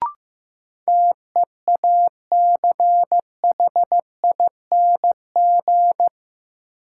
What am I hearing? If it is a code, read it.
Morse code, heard as a keyed tone, about 16 wpm, 700 Hz. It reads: TEACHING
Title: Sentences from Top 1100 Words - New Word Drill - 15wpm